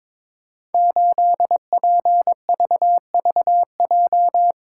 Morse code, heard as a keyed tone, about 21 wpm, 700 Hz. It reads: 8PVVJ
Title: Code Groups - 5 Letter:Numbers - 22wpm